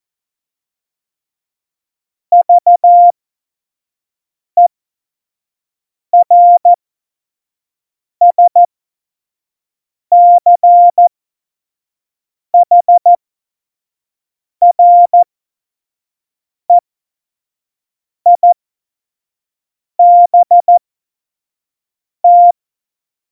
Morse code, heard as a keyed tone, about 14 wpm, spaced as for 5 wpm, 700 Hz. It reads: VERSCHREIBT
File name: Morsekurs / Witze / Witz 21 Juli 2020, 25 F70